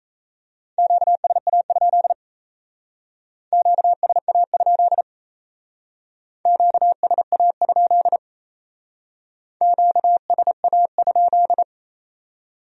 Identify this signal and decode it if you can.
Morse code, heard as a keyed tone, about 34 wpm, 700 Hz. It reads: QSA? QSA? QSA? QSA?
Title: Speed-Racing Q-Codes - 28wpm